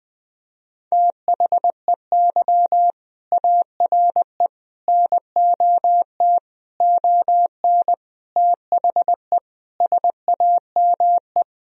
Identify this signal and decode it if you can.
Morse code, heard as a keyed tone, about 21 wpm, 700 Hz. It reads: THEY ARE NOT ON THE SAME
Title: Sentences from Top 1000 Words - Review Part 4 - 20wpm